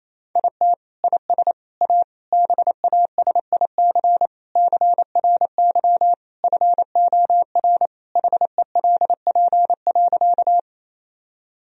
Morse code, heard as a keyed tone, about 28 wpm, 700 Hz. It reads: IT IS A BASIC CRY FOR HELP.
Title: Sentences from Top 1000 Words - Review Part 3 - 28wpm